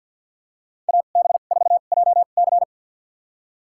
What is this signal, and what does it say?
Morse code, heard as a keyed tone, about 52 wpm, spaced as for 36 wpm, 690 Hz. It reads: AB4JC
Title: USA Call Signs - 2x Character Spacing - 50wpm